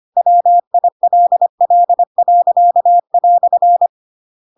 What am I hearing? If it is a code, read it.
Morse code, heard as a keyed tone, about 25 wpm, 700 Hz. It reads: WILL."
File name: Pinocchio / Pinocchio ch8 25 25 ik1hge